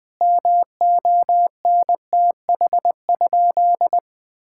Morse code, heard as a keyed tone, about 20 wpm, 700 Hz. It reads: MONTH?
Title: Sentences from Top 1000 Words - Review Part 5 - 20wpm